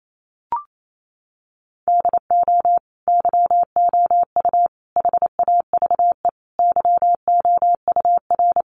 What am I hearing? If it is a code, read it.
Morse code, heard as a keyed tone, about 28 wpm, 700 Hz. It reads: DO YOU HAVE YOUR
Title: Sentences from Top 1400 Words 28wpm